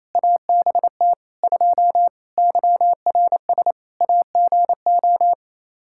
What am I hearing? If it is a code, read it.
Morse code, heard as a keyed tone, about 28 wpm, 700 Hz. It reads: ABT 2 YRS AGO
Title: K1USN SST QSO Practice 28wpm